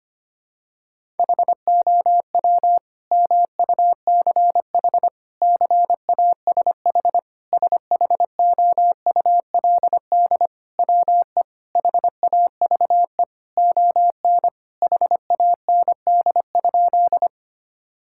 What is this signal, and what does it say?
Morse code, heard as a keyed tone, about 25 wpm, 700 Hz. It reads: HOW MUCH CASH SHOULD WE HAVE ON HAND?
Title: Sentences from Top 1000 Words - Review Part 5 - 25wpm